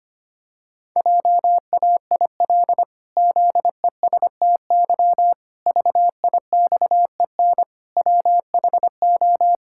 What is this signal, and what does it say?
Morse code, heard as a keyed tone, about 25 wpm, 700 Hz. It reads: JAIL ZESTY VIXEN WHO
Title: Pangram Challenge v2 - 25wpm